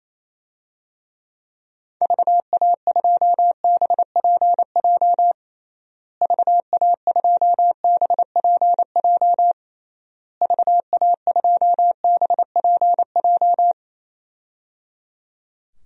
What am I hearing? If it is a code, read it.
Morse code, heard as a keyed tone, about 28 wpm, 700 Hz. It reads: VA2BPJ VA2BPJ VA2BPJ
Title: DXCC-100 Call Signs - 6 Characters - Repeated 3x - 28wpm